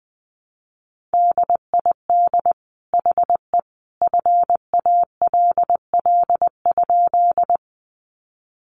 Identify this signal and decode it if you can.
Morse code, heard as a keyed tone, about 20 wpm, 700 Hz. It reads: DID HE FALL?
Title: Sentences from Top 500 Words 20wpm